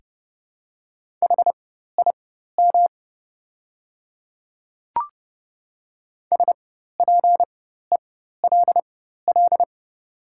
Morse code, heard as a keyed tone, about 31 wpm, spaced as for 14 wpm, 700 Hz. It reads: HIM SPELL
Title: 4x Character Spacing - Top 200 Words - 30wpm